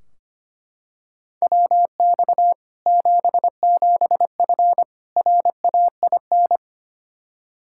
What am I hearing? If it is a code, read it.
Morse code, heard as a keyed tone, about 25 wpm, 700 Hz. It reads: WX 77F RAIN
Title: Practice QSO - Exchange 2 - Info - 25wpm